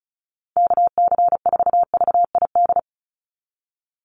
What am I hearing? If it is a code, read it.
Morse code, heard as a keyed tone, about 34 wpm, 700 Hz. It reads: KC4VID
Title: USA Call Signs 35wpm